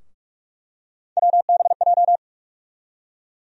Morse code, heard as a keyed tone, about 42 wpm, 700 Hz. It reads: WBJ
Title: USA Call Sign Postfixes - 3 Letters 45wpm